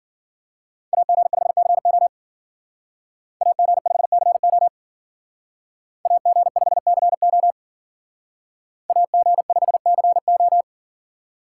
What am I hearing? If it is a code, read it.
Morse code, heard as a keyed tone, about 48 wpm, 700 Hz. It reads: AG5CO AG5CO AG5CO AG5CO
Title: Speed Racing - USA 2x2 Call Signs 40wpm